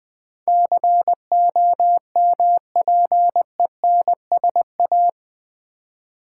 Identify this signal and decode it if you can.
Morse code, heard as a keyed tone, about 20 wpm, 700 Hz. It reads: COMPENSA
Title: Build A Word - 4 Letters At A Time - 10-Letter Words - 20wpm